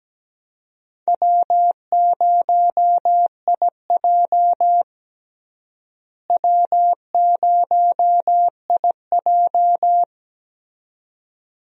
Morse code, heard as a keyed tone, about 17 wpm, 700 Hz. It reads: W0IJ W0IJ
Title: USA Call Signs - 4 Characters - Repeated 2x - 17wpm